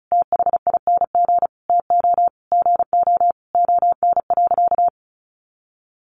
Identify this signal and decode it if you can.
Morse code, heard as a keyed tone, about 34 wpm, 700 Hz. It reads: THING TO GO ON.
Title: Sentences from Top 1300 Words 35wpm